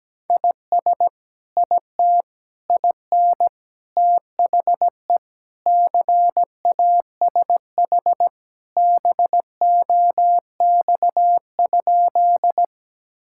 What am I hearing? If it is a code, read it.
Morse code, heard as a keyed tone, about 17 wpm, 700 Hz. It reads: IS IT IN THE CASH BOX?
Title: Sentences from Top 1000 Words - Review Part 5 - 17wpm